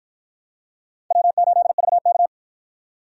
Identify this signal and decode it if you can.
Morse code, heard as a keyed tone, about 49 wpm, 700 Hz. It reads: W8VK